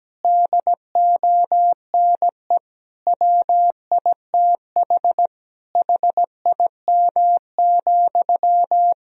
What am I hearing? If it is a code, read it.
Morse code, heard as a keyed tone, about 17 wpm, 700 Hz. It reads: DONE WITH HIM,
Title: Sentences from Top 1000 Words - Review Part 2 - 17wpm